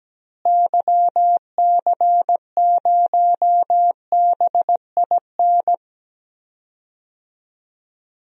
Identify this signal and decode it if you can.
Morse code, heard as a keyed tone, about 17 wpm, 700 Hz. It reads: YC0BIN
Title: DXCC-100 Call Signs - 6 Characters - Repeated 3x - 17wpm